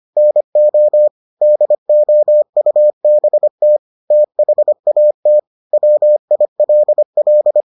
Morse code, heard as a keyed tone, about 25 wpm, 600 Hz. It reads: NO DOUBT THAT WILL